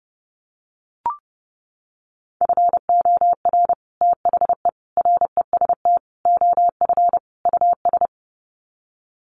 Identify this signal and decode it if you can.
Morse code, heard as a keyed tone, about 30 wpm, 700 Hz. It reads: FOR THE REST OF US
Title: Sets of 5 Words Encore 30wpm